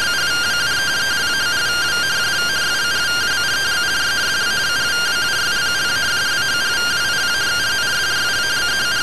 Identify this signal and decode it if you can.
Two-tone teleprinter signal, about 170 Hz shift.
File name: findings / MR SAGGITAL fm10 12mm 6mm.gen-21 rtty rx